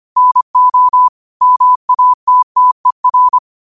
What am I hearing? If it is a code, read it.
Morse code, heard as a keyed tone, about 25 wpm, 1000 Hz. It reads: NO MATTER
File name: qotdfiles / 25wpm